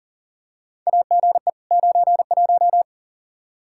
Morse code, heard as a keyed tone, about 40 wpm, 700 Hz. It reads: AGE 91
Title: Practice QSO - Exchange 3 - More Info - 40wpm